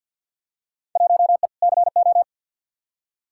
Morse code, heard as a keyed tone, about 46 wpm, 700 Hz. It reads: 1E CO
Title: Field Day Exchanges - 50wpm